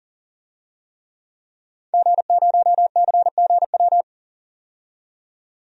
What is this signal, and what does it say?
Morse code, heard as a keyed tone, about 40 wpm, 700 Hz. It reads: G0CGW
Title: DXCC-100 Call Signs - 5 Characters - 40wpm